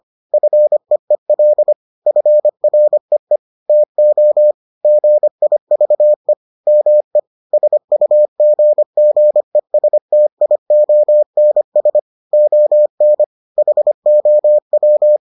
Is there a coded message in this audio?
Yes: Morse code from 0.3 to 15.2 s, heard as a keyed tone, about 25 wpm, 600 Hz. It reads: FEEL FREE TO GIVE ME SUGGESTIONS ON HOW